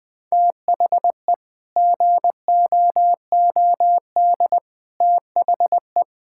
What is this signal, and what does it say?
Morse code, heard as a keyed tone, about 21 wpm, 700 Hz. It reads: THE GOOD THE
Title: Sets of 6 Words - 20wpm